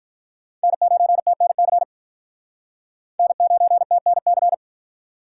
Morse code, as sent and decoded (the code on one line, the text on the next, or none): -. ----. - -. -.-. / -. ----. - -. -.-.
N9TNC N9TNC